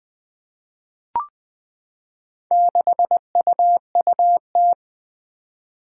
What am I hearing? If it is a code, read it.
Morse code, heard as a keyed tone, about 20 wpm, 700 Hz. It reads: 6UUT
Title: Four Challenging Letter-Number Combinations 20wpm